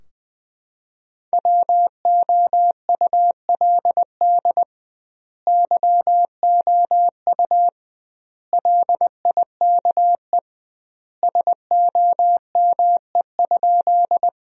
Morse code, heard as a keyed tone, about 20 wpm, 700 Hz. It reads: WOULD YOU LIKE SOME?
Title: 2x Word-Spacing - Sentences from Top 100 Words - 20wpm